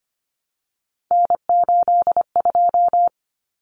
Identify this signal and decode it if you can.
Morse code, heard as a keyed tone, about 25 wpm, 700 Hz. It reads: N82
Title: Common Cut Numbers - 3-Digit 25wpm